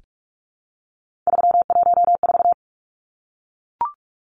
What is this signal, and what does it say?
Morse code, heard as a keyed tone, about 44 wpm, 700 Hz. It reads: 314
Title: Numbers - 3-Digit 45wpm